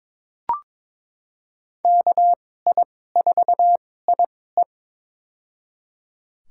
Morse code, heard as a keyed tone, about 22 wpm, spaced as for 16 wpm, 700 Hz. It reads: KI4IE
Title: USA Call Signs - 5 Characters - 2x Character Spacing - 22wpm